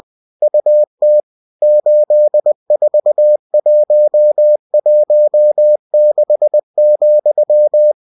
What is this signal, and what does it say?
Morse code, heard as a keyed tone, about 20 wpm, 600 Hz. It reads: UT 84116,
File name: cw-books / winnie-the-pooh / winnie-20wpm-page-0121